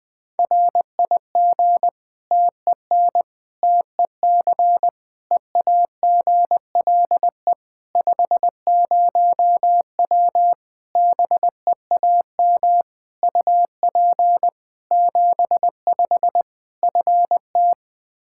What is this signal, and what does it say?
Morse code, heard as a keyed tone, about 20 wpm, 700 Hz. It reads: RIG TEN TEC EAGLE 50W BEAM UP 75 FT
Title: Practice QSO - Exchange 2 - Info - 20wpm